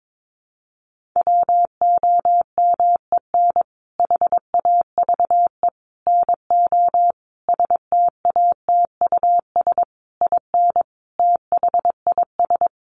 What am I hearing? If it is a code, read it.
Morse code, heard as a keyed tone, about 22 wpm, 700 Hz. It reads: WOMEN HAVE NO STATUS IN THIS
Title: Sentences from Top 1400 Words 22wpm